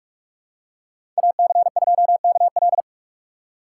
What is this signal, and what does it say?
Morse code, heard as a keyed tone, about 43 wpm, 700 Hz. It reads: AC2KL
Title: USA Call Signs - 5 Characters - 2x Character Spacing - 45wpm